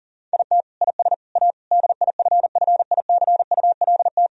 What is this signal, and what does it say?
Morse code, heard as a keyed tone, about 37 wpm, 690 Hz. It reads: IT IS A DIFFICULT
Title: Sentences from Top 1000 Words - Review Part 4 - 40wpm